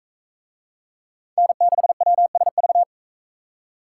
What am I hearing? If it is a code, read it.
Morse code, heard as a keyed tone, about 42 wpm, 700 Hz. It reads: N6WSV